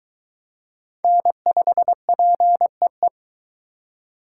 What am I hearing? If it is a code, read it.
Morse code, heard as a keyed tone, about 22 wpm, 700 Hz. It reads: N5PEE